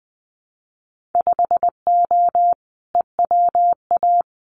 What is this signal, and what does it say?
Morse code, heard as a keyed tone, about 20 wpm, 700 Hz. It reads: 5O EWA
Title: Winter Field Day Exchanges Encore 20wpm